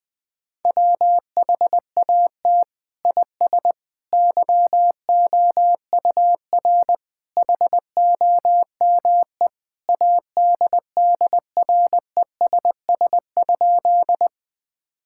Morse code, heard as a keyed tone, about 21 wpm, 700 Hz. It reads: WHAT IS YOUR HOME ADDRESS?
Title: Sentences from Top 1000 Words - Review Part 6 - 20wpm